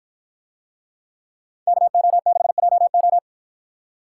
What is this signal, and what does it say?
Morse code, heard as a keyed tone, about 53 wpm, 700 Hz. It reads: KO6JO